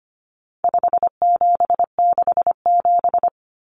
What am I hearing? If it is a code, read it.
Morse code, heard as a keyed tone, about 25 wpm, 700 Hz. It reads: 5767
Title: Common Cut Numbers - 4-Digit 25wpm